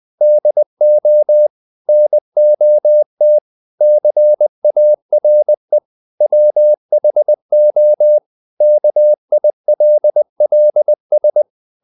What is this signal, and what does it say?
Morse code, heard as a keyed tone, about 20 wpm, 600 Hz. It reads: DO NOT CARE WHO KILLS
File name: the old man and the sea part 2 20wpm